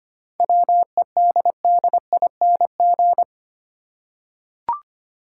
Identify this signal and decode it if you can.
Morse code, heard as a keyed tone, about 25 wpm, 700 Hz. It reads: WEDDING
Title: Seven Letter Words - 25wpm